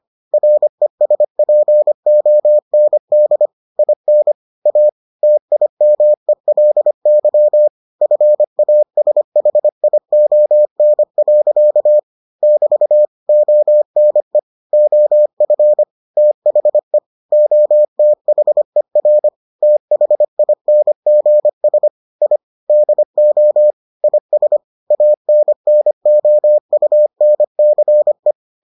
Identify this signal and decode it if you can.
Morse code, heard as a keyed tone, about 25 wpm, 600 Hz. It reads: RESPOND IN A TIMELY FASHION. <BT> ONE OF THE OTHER THINGS I DO IS ANNOUNCE